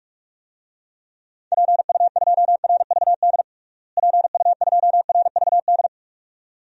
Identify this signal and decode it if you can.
Morse code, heard as a keyed tone, about 41 wpm, 700 Hz. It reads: PU2RVD PU2RVD
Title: DXCC-100 Call Signs - 6 Characters - Repeated 2x - 45wpm